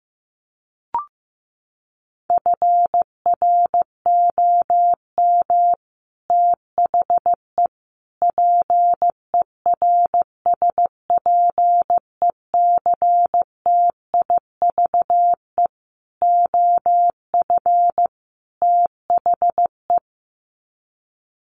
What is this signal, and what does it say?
Morse code, heard as a keyed tone, about 15 wpm, 700 Hz. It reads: FROM THE PERSPECTIVE OF THE
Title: Sets of 5 Words Encore2 15wpm